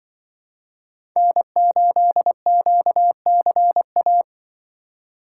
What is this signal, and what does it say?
Morse code, heard as a keyed tone, about 24 wpm, 700 Hz. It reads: N8QCA
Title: Speed Racing - USA 1x3 Call Signs 20wpm